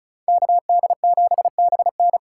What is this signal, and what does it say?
Morse code, heard as a keyed tone, about 43 wpm, 700 Hz. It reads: KD7BN
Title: Speed Racing - USA 2x2 Call Signs 35wpm